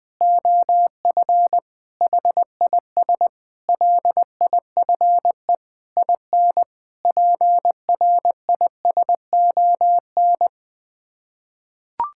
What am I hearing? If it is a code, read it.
Morse code, heard as a keyed tone, about 20 wpm, 700 Hz. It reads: OF HIS LIFE IN PRISON
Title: Sets of 8 Words - 20wpm